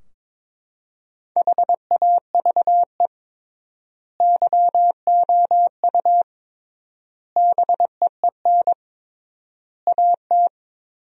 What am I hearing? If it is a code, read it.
Morse code, heard as a keyed tone, about 22 wpm, 700 Hz. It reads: HAVE YOU BEEN AT
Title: 3x Word-Spacing - Sentences from Top 300 Words - 22wpm